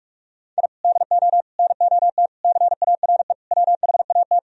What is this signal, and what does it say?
Morse code, heard as a keyed tone, about 44 wpm, 700 Hz. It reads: I DO NOT CARE WHAT